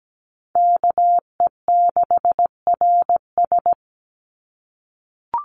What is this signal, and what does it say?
Morse code, heard as a keyed tone, about 17 wpm, 700 Hz. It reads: KE6RS
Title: NA SOTA Chaser Callsigns 17wpm